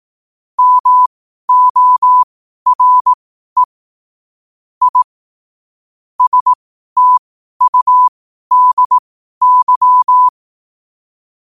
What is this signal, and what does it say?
Morse code, heard as a keyed tone, about 18 wpm, spaced as for 12 wpm, 1000 Hz. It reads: MORE I STUDY
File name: qotdfiles / 12wpm